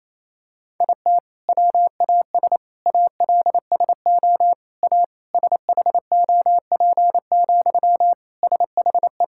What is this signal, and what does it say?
Morse code, heard as a keyed tone, about 28 wpm, 700 Hz. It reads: IT WAS ALSO A SHOP, SHE
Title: Sentences from Top 1600 Words 28wpm